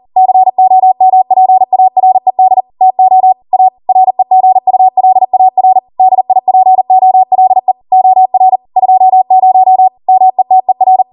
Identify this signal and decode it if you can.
Morse code, heard as a keyed tone, about 40 wpm, 750 Hz. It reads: COMPARED TO A REGULAR DIPOLE OR 20 METER